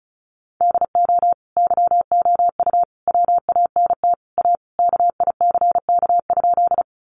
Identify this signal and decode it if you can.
Morse code, heard as a keyed tone, about 34 wpm, 700 Hz. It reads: DO YOU WANT A KICK?
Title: Sentences from Top 1300 Words 35wpm